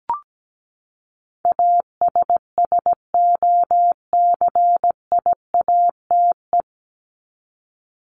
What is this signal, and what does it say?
Morse code, heard as a keyed tone, about 17 wpm, 700 Hz. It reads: ASSOCIATE
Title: Top 1751-2000 Words 17wpm